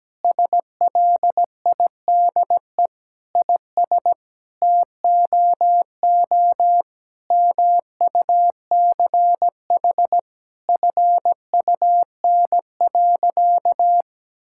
Morse code, heard as a keyed tone, about 17 wpm, 700 Hz. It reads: SLIDE IS TOO MUCH FUN.
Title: Sentences from Top 1600 Words 17wpm